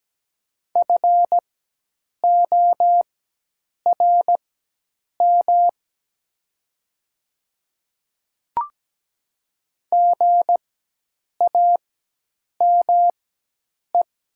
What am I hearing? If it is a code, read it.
Morse code, heard as a keyed tone, about 17 wpm, spaced as for 8 wpm, 700 Hz. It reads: FORM GAME
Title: 4x Character Spacing - Top 500 Words - 17wpm